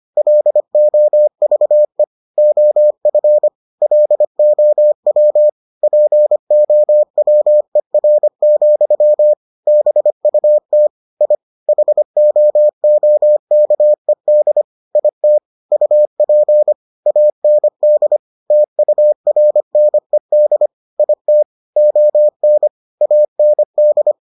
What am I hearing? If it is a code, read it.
Morse code, heard as a keyed tone, about 25 wpm, 600 Hz. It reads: LOVE OF LOW POWER, BUT I HOOKED IT UP AND TURNED IT ON AND